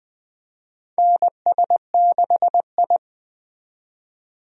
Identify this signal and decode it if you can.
Morse code, heard as a keyed tone, about 20 wpm, 700 Hz. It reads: NS6I
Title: USA Call Signs - 4 Characters - 20wpm